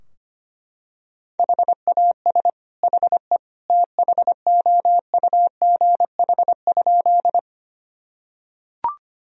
Morse code, heard as a keyed tone, about 25 wpm, 700 Hz. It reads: HAS HE THOUGH?
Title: Sentences from Top 1000 Words - Review Part 2 - 25wpm